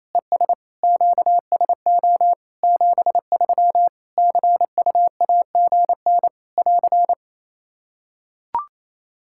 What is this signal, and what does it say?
Morse code, heard as a keyed tone, about 28 wpm, 700 Hz. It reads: ES QSO 73 CUAGN <AR>
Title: K1USN SST QSO Practice 28wpm